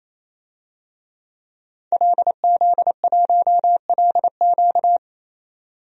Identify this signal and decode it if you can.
Morse code, heard as a keyed tone, about 28 wpm, 700 Hz. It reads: LZ1LQ
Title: DXCC-100 Call Signs - 5 Characters - 28wpm